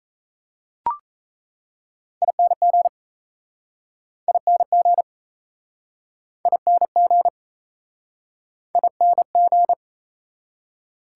Morse code, heard as a keyed tone, about 38 wpm, 700 Hz. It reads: ING ING ING ING